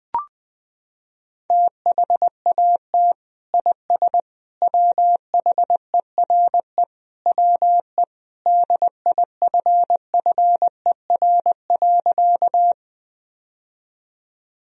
Morse code, heard as a keyed tone, about 21 wpm, 700 Hz. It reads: THAT IS WHERE WE DIFFER.
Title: Sentences from Top 1000 Words - Review Part 1 - 20wpm